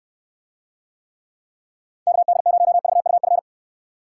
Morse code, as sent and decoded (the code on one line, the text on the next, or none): -.- -.. ----- ...- ..- ..-.
KD0VUF